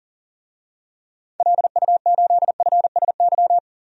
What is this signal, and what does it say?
Morse code, heard as a keyed tone, about 37 wpm, 700 Hz. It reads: LU8FSY
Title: DXCC-100 Call Signs - 6 Characters - Repeated 2x - 40wpm